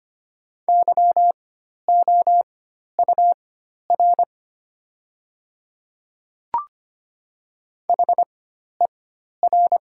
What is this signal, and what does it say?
Morse code, heard as a keyed tone, about 25 wpm, spaced as for 12 wpm, 700 Hz. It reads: YOUR HER
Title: 4x Character Spacing - Top 100 Words - 25wpm